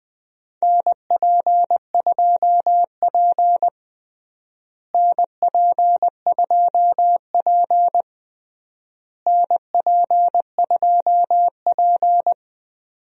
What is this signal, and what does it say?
Morse code, heard as a keyed tone, about 20 wpm, 700 Hz. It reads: NP2P NP2P NP2P
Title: USA Call Signs - 4 Characters - Repeated 3x - 20wpm